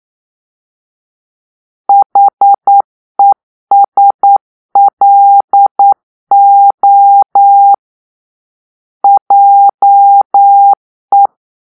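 Morse code, heard as a keyed tone, about 9 wpm, 800 Hz. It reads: HESLO JE
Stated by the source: Morse-code cipher made by a player team for the Kachna puzzle game.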